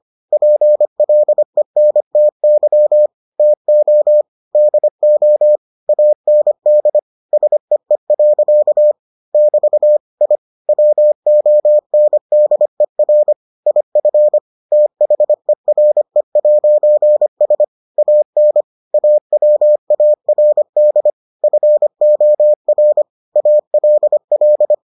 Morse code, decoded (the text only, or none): PLENTY TO DO AND SEE. <BT> I WONDER IF THERE'S AN AWARD FOR ALL